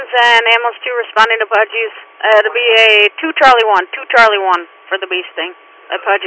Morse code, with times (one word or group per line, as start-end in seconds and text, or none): none